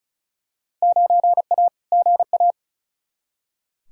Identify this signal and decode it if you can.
Morse code, heard as a keyed tone, about 36 wpm, 700 Hz. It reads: 9A GA
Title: Field Day Exchanges Encore - 35wpm